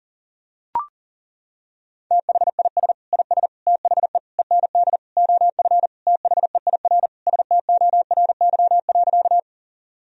Morse code, heard as a keyed tone, about 40 wpm, 700 Hz. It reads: THIS IS THE END OF THEIR STORY.
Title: Sentences from Top 1000 Words - Review Part 1 - 40wpm